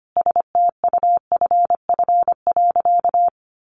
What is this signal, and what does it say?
Morse code, heard as a keyed tone, about 25 wpm, 700 Hz. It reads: STUFF.
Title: Sentences from Top 1400 Words 25wpm